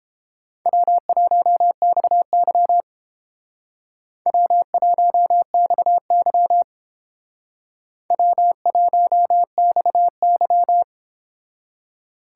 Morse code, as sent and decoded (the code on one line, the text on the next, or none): .-- .---- -..- -.-- / .-- .---- -..- -.-- / .-- .---- -..- -.--
W1XY W1XY W1XY